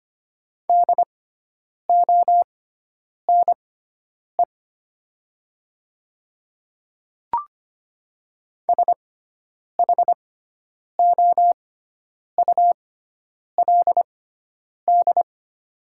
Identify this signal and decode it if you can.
Morse code, heard as a keyed tone, about 25 wpm, spaced as for 9 wpm, 700 Hz. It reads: DONE SHOULD
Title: 6x Character Spacing - Top 500 Words - 25wpm